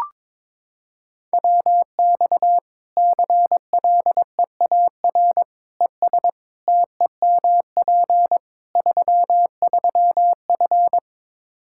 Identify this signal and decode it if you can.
Morse code, heard as a keyed tone, about 22 wpm, 700 Hz. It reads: WX CLEAR ES TEMP 33F
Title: Practice QSO - Exchange 2 - Info - 22wpm